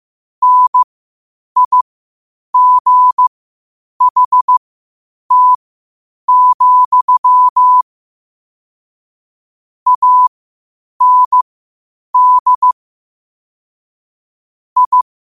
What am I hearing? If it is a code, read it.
Morse code, heard as a keyed tone, about 15 wpm, spaced as for 8 wpm, 1000 Hz. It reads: NIGHT, AND I
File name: qotdfiles / 8wpm